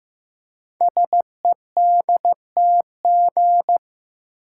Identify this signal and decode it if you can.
Morse code, heard as a keyed tone, about 15 wpm, 700 Hz. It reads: SEDTG
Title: Code Groups - 5 Letter:Numbers - 15wpm